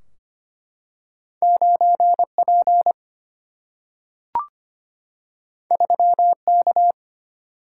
Morse code, heard as a keyed tone, about 25 wpm, 700 Hz. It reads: 9P 3K
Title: Single-Letter Single-Number 25wpm